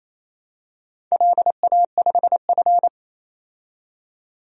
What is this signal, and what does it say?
Morse code, heard as a keyed tone, about 28 wpm, 700 Hz. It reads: LA5F
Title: DXCC-100 Call Signs - 4 Characters - 28wpm